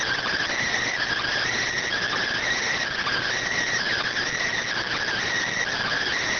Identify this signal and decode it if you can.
Emergency Alert System (SAME) tones (header bursts), heard as harsh buzzing bursts. It is heard from 0.8 to 5.2 s.